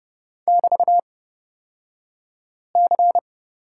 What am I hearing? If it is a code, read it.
Morse code, heard as a keyed tone, about 30 wpm, spaced as for 5 wpm, 700 Hz. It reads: <BT>C
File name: Lesson 36d - BT - Two Characters 30wpm